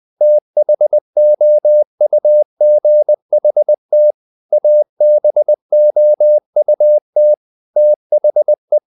Morse code, heard as a keyed tone, about 20 wpm, 600 Hz. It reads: THOUGHT ABOUT THE
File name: the old man and the sea / the old man and the sea part 1 20wpm